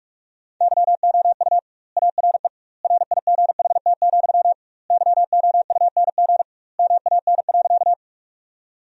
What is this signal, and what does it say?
Morse code, heard as a keyed tone, about 46 wpm, 700 Hz. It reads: YOU ARE RIGHT, YOUNG MAN.
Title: 2x Word-Spacing - Sentences from Top 400 Words - 45wpm